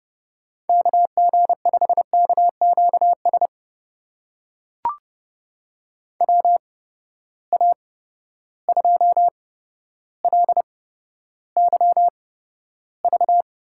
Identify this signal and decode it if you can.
Morse code, heard as a keyed tone, about 31 wpm, spaced as for 13 wpm, 700 Hz. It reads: KG5KQS WA2LYV
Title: USA Call Signs - 6 Characters - 8x Character Spacing - 30wpm